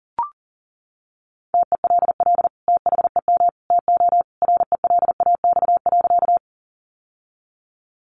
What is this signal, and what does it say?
Morse code, heard as a keyed tone, about 38 wpm, 700 Hz. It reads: TELL THEM TO RELAX.
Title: Sentences from Top 1400 Words 40wpm